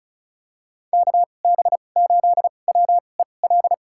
Morse code, heard as a keyed tone, about 36 wpm, spaced as for 25 wpm, 700 Hz. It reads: KB8WEL
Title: USA Call Signs - 2x Character Spacing - 35wpm